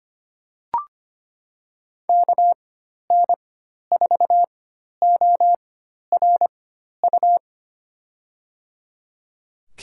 Morse code, heard as a keyed tone, about 25 wpm, spaced as for 12 wpm, 700 Hz. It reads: KN4ORU
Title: USA Call Signs - 4x Character Spacing - 25wpm